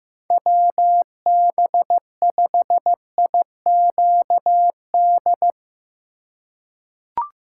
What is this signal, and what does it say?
Morse code, heard as a keyed tone, about 15 wpm, 700 Hz. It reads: WB5IQD
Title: USA Call Signs - 6 Characters - 3x Character Spacing - 15wpm